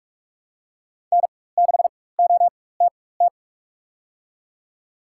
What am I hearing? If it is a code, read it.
Morse code, heard as a keyed tone, about 48 wpm, spaced as for 21 wpm, 700 Hz. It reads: N6OTT
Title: USA Call Signs - 4x Character Spacing - 45wpm